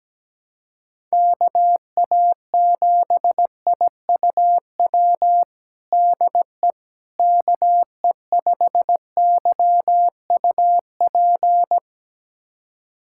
Morse code, heard as a keyed tone, about 17 wpm, 700 Hz. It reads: KA7IUW DE KE5YUP
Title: USA Callsign DE Callsign - 6 Characters - 17wpm